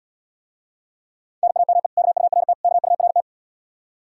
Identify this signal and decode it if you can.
Morse code, heard as a keyed tone, about 80 wpm, 700 Hz. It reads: NAME KURT KURT